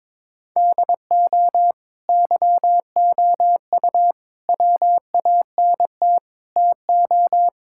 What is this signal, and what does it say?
Morse code, heard as a keyed tone, about 22 wpm, 700 Hz. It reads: DO YOU WANT TO
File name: practice / 22wpm / Sentences with 3-Word Phrases - 22wpm